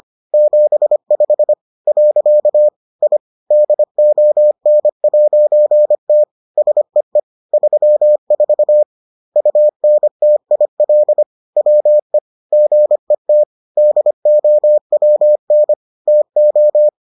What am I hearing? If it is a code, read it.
Morse code, heard as a keyed tone, about 25 wpm, 600 Hz. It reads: 75 . I DON'T SEE 34 UNTIL WE GET DOWN TO